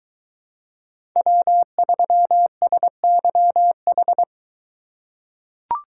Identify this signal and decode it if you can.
Morse code, heard as a keyed tone, about 22 wpm, 700 Hz. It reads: W3SYH